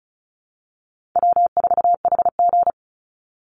Morse code, heard as a keyed tone, about 34 wpm, 700 Hz. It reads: W4HG
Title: Call Signs Repeated 3x - 35wpm